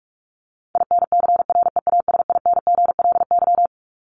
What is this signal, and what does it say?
Morse code, heard as a keyed tone, about 44 wpm, 700 Hz. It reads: INCREASINGLY
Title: LY Suffix - Top 100 Words with Base Words 45wpm